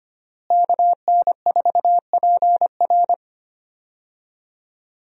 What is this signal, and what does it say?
Morse code, heard as a keyed tone, about 25 wpm, 700 Hz. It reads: KN4PR
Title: USA Call Signs - 5 Characters - Repeated 2x - 25wpm